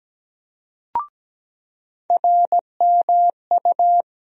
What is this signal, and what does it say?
Morse code, heard as a keyed tone, about 17 wpm, 700 Hz. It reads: RMU
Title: USA Call Sign Postfixes - 3 Letters 17wpm